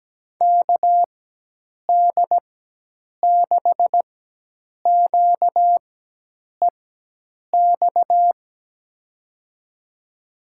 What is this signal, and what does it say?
Morse code, heard as a keyed tone, about 17 wpm, spaced as for 8 wpm, 700 Hz. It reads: KD6QEX
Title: USA Call Signs - 6 Characters - 4x Character Spacing - 17wpm